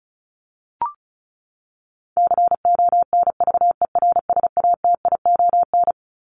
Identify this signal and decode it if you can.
Morse code, heard as a keyed tone, about 34 wpm, 700 Hz. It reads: CONVERSATION
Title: Top 1751-2000 Words 35wpm